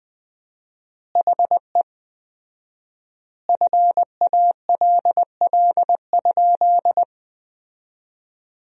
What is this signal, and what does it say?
Morse code, heard as a keyed tone, about 20 wpm, 700 Hz. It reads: HE FALL?
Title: 4x Word-Spacing - Sentences from Top 500 Words - 20wpm